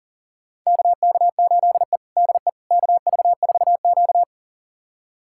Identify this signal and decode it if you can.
Morse code, heard as a keyed tone, about 37 wpm, 700 Hz. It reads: KK8E DE KV4Q
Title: USA Callsign DE Callsign - 4 Characters - 40wpm